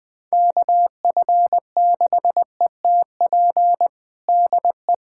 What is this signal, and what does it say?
Morse code, heard as a keyed tone, about 20 wpm, 700 Hz. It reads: KF6ETP DE
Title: USA Callsign DE Callsign - 6 Characters - 20wpm